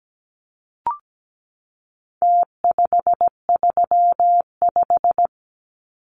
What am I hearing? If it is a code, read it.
Morse code, heard as a keyed tone, about 17 wpm, 700 Hz. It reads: T535
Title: Numbers - Contest Style - 4-Digit 17wpm